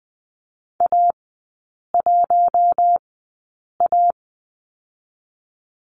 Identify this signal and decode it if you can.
Morse code, heard as a keyed tone, about 20 wpm, spaced as for 8 wpm, 700 Hz. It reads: A1A
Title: 047 Lesson 9e - Number 4 - Three Characters 20wpm